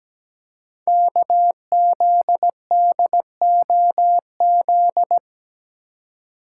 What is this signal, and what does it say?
Morse code, heard as a keyed tone, about 17 wpm, 700 Hz. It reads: KZDOZ
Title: Code Groups - 5 Letter:Numbers - 17wpm